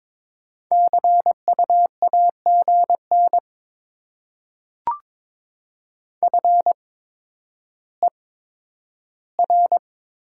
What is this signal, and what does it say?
Morse code, heard as a keyed tone, about 22 wpm, spaced as for 11 wpm, 700 Hz. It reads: CU AGN FER
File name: Top 20 QSO Elements - 8x Character Spacing - 22wpm